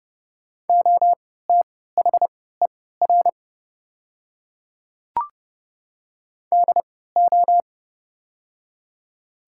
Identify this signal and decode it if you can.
Morse code, heard as a keyed tone, about 30 wpm, spaced as for 17 wpm, 700 Hz. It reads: OTHER DO
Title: 3x Character Spacing - Top 300 Words - 30wpm